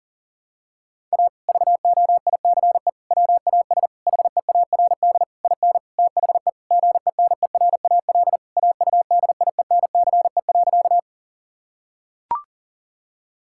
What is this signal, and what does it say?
Morse code, heard as a keyed tone, about 37 wpm, 700 Hz. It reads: A VOICE WAS HEARD IN THE GENERAL AUDIENCE.
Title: Sentences from Top 1000 Words - Review Part 5 - 40wpm